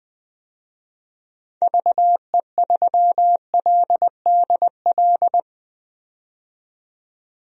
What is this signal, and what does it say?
Morse code, heard as a keyed tone, about 20 wpm, 700 Hz. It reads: VE3LDL
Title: DXCC-100 Call Signs - 6 Characters - 20wpm